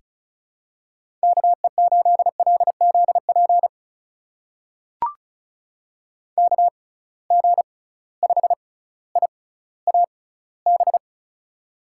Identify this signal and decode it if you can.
Morse code, heard as a keyed tone, about 35 wpm, spaced as for 18 wpm, 700 Hz. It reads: KE8LZP KG5IAB
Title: USA Call Signs - 6 Characters - 6x Character Spacing - 35wpm